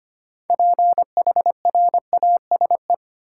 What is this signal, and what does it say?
Morse code, heard as a keyed tone, about 25 wpm, 700 Hz. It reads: PHRASE
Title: Six Letter Words - 25wpm